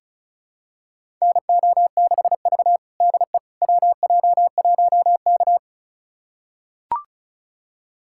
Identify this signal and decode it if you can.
Morse code, heard as a keyed tone, about 36 wpm, 690 Hz. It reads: NO6V DE WJ1K
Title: USA Callsign DE Callsign - 4 Characters - 35wpm